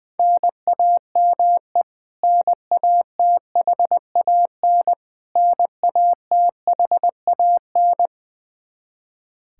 Morse code, heard as a keyed tone, about 21 wpm, 700 Hz. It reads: NAME NATHAN NATHAN
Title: Practice QSO - Exchange 1 - Beginning - 20wpm